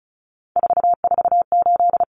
Morse code, heard as a keyed tone, about 34 wpm, 700 Hz. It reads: 448
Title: Common Cut Numbers - 3-Digit 35wpm